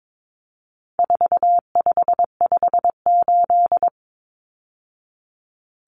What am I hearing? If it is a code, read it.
Morse code, heard as a keyed tone, about 22 wpm, 700 Hz. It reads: 4558
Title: Numbers - 4-Digit 22wpm